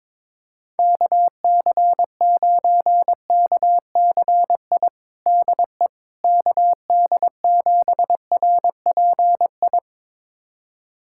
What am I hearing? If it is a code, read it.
Morse code, heard as a keyed tone, about 21 wpm, 700 Hz. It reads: KC9KCI DE KD7RPI
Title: USA Callsign DE Callsign 22wpm